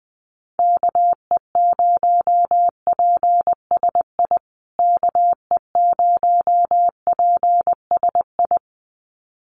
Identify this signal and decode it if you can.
Morse code, heard as a keyed tone, about 19 wpm, 700 Hz. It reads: KE0PSI KE0PSI
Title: USA Call Signs Repeated 2x 20wpm